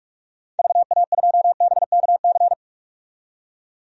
Morse code, heard as a keyed tone, about 46 wpm, 700 Hz. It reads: VA2BKC
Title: DXCC-100 Call Signs - 6 Characters - 45wpm